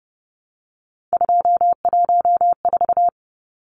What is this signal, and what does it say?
Morse code, heard as a keyed tone, about 30 wpm, 700 Hz. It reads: 214
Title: Numbers - 3-Digit 30wpm